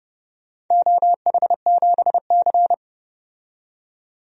Morse code, heard as a keyed tone, about 28 wpm, 700 Hz. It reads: OH7C
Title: DXCC-100 Call Signs - 4 Characters - 30wpm